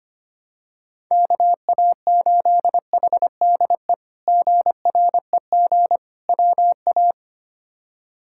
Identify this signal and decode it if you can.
Morse code, heard as a keyed tone, about 25 wpm, 700 Hz. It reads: KA8HDE GREG WA
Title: K1USN SST QSO Practice - 25wpm